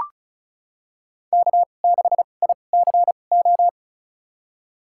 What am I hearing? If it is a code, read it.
Morse code, heard as a keyed tone, about 36 wpm, spaced as for 25 wpm, 700 Hz. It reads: K6ICO
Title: USA Call Signs - 5 Characters - 2x Character Spacing - 35wpm